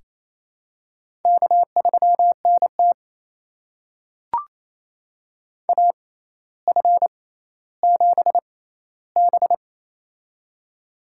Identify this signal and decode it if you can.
Morse code, heard as a keyed tone, about 28 wpm, spaced as for 14 wpm, 700 Hz. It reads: K3NT AF7B
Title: USA Call Signs - 4 Characters - 6x Character Spacing - 28wpm